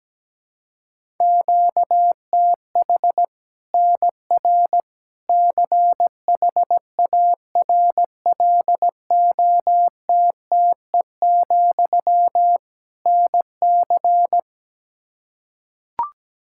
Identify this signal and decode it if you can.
Morse code, heard as a keyed tone, about 17 wpm, 700 Hz. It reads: QTH NR CHARLOTTE, NC
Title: Practice QSO - Exchange 1 - Beginning - 17wpm